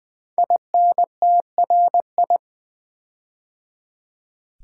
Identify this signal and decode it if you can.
Morse code, heard as a keyed tone, about 20 wpm, 700 Hz. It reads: INTRI
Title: Build A Word - 5 Letters At A Time - 10-Letter Words - 20wpm